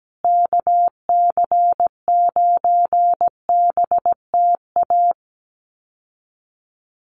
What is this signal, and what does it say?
Morse code, heard as a keyed tone, about 17 wpm, 700 Hz. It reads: KC9BTA
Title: USA Call Signs Repeated 2x 17wpm